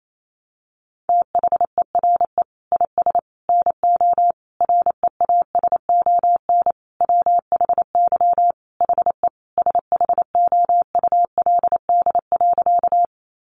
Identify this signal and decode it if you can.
Morse code, heard as a keyed tone, about 28 wpm, 700 Hz. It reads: THERE IS NO REASON WHY HE SHOULD.
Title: Sentences from Top 700 Words 28wpm